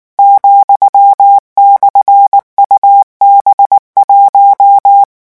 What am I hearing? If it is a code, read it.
Morse code, heard as a keyed tone, about 19 wpm, 800 Hz. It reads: ,/UB1
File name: morse mp3 / revision15-vitesse 18 mots